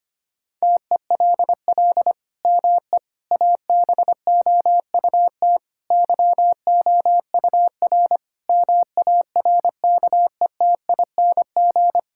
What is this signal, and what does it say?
Morse code, heard as a keyed tone, about 25 wpm, 700 Hz. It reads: TELL ME ABOUT YOUR MARKETING
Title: Sentences from Top 1000 Words - Review Part 5 - 25wpm